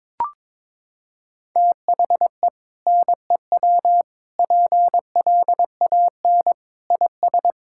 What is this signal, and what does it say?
Morse code, heard as a keyed tone, about 22 wpm, 700 Hz. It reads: THE NEW PLAN IS
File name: practice / 22wpm / Sentences with 2-Word Phrases - 22wpm